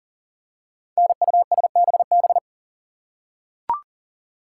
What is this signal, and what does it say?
Morse code, heard as a keyed tone, about 43 wpm, 700 Hz. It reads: NUSBB
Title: Code Groups - 5 Letters - 40wpm